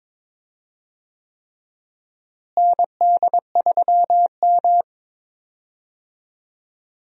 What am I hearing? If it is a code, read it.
Morse code, heard as a keyed tone, about 22 wpm, 700 Hz. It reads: ND3M
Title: USA Call Signs - 4 Characters - Repeated 3x - 22wpm